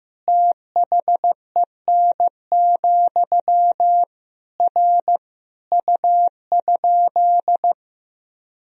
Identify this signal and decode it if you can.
Morse code, heard as a keyed tone, about 15 wpm, 700 Hz. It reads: THEN, R U?
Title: Sentences from Top 100 Words and Abbreviations 15wpm